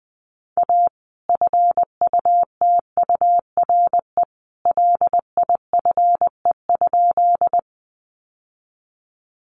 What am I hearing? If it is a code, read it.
Morse code, heard as a keyed tone, about 20 wpm, 700 Hz. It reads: A FUTURE LIFE?
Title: Sentences from Top 700 Words 20wpm